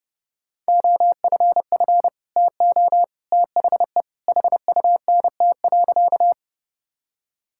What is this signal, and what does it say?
Morse code, heard as a keyed tone, about 32 wpm, 700 Hz. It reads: OFF TO THE HUNT.
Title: Sentences from Top 1600 Words 30wpm